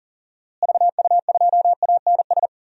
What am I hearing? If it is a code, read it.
Morse code, heard as a keyed tone, about 39 wpm, 700 Hz. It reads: VU2ANS
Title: DXCC-100 Call Signs - 6 Characters - Repeated 3x - 40wpm